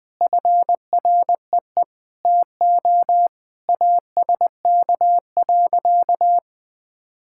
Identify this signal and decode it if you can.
Morse code, heard as a keyed tone, about 20 wpm, 700 Hz. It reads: FREE TO ASK.
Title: Sentences with 8-Word Phrases - 20wpm